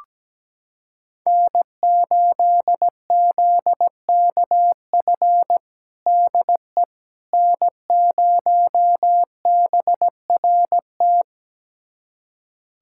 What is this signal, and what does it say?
Morse code, heard as a keyed tone, about 17 wpm, 700 Hz. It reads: N8ZKF DE N0BRT
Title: USA Callsign DE Callsign - 5 Characters - 17wpm